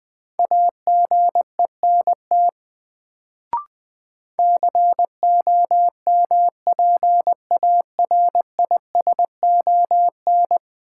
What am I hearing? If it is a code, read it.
Morse code, heard as a keyed tone, about 20 wpm, 700 Hz. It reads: AGENT COMPARISON
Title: Sentences from Top 1600 Words - New Word Drill - 20wpm